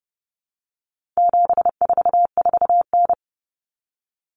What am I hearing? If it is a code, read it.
Morse code, heard as a keyed tone, about 30 wpm, 700 Hz. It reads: 744N
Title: Common Cut Numbers - 4-Digit 30wpm